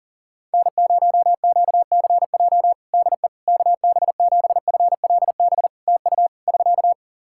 Morse code, heard as a keyed tone, about 40 wpm, 700 Hz. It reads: N0QCJ DE KB7FLB TU <SK>
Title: Practice QSO - Exchange 4 - Ending - 40wpm